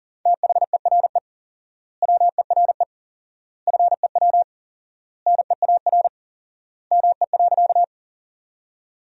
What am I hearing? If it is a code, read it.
Morse code, heard as a keyed tone, about 40 wpm, 700 Hz. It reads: THERE WERE FEW NEAR ME.
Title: 4x Word-Spacing - Sentences from Top 300 Words - 40wpm